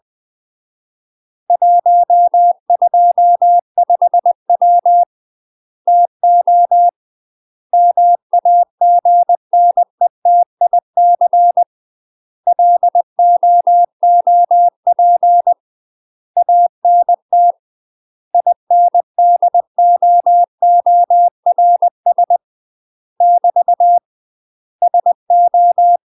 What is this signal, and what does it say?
Morse code, heard as a keyed tone, about 20 wpm, 700 Hz. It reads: 125W TO MAGNETIC LOOP ANT INDOORS <BT> SO
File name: UK MASTER QSOs TRAINING / 20WPM / UK MASTER QSOs 0039